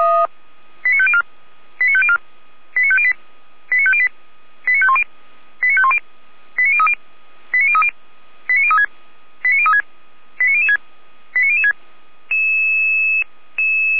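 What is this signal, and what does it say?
Five-tone selective call (ZVEI), heard as quick runs of five tones, tones of about 70 ms. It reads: 89504 89504 89508 89508 89510 89510 89030 89030 89036 89036 89006 89006
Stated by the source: ZVEI five-tone call received on an SDR, recorded by the Feuerwehr Orlamünde fire brigade.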